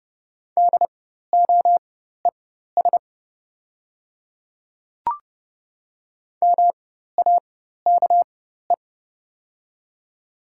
Morse code, heard as a keyed tone, about 30 wpm, spaced as for 14 wpm, 700 Hz. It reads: DOES MAKE
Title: 4x Character Spacing - Top 200 Words - 30wpm